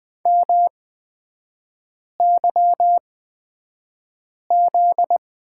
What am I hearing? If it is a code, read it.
Morse code, heard as a keyed tone, about 20 wpm, spaced as for 5 wpm, 700 Hz. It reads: MYZ
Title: 243 Lesson 39e - Letter z - Three Characters 20wpm